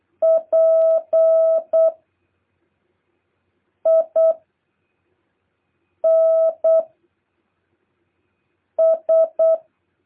Morse code, heard as a keyed tone, about 8 wpm, 650 Hz. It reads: P I N S
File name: jh8cbh / 2023cw